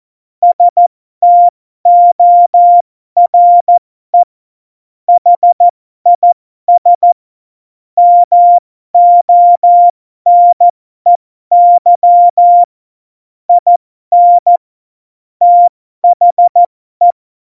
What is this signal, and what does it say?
Morse code, heard as a keyed tone, about 14 wpm, spaced as for 12 wpm, 700 Hz. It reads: STORE HIS MONEY IN THE
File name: Morsekurs / Witze